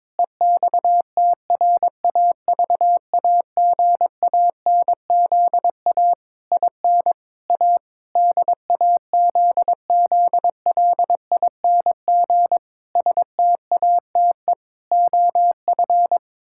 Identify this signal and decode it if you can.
Morse code, heard as a keyed tone, about 21 wpm, 700 Hz. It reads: EXTRAVAGANZA IN A DAZZLING STATE OF
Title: Pangram Challenge v2 - 22wpm